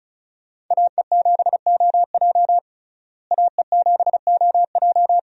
Morse code, heard as a keyed tone, about 34 wpm, 700 Hz. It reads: AE7OJ AE7OJ
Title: USA Call Signs - 5 Characters - Repeated 3x - 35wpm